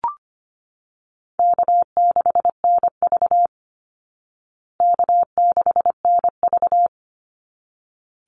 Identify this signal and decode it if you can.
Morse code, heard as a keyed tone, about 25 wpm, 700 Hz. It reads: K6NV K6NV
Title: Easy Call Signs Repeated 2x - 25wpm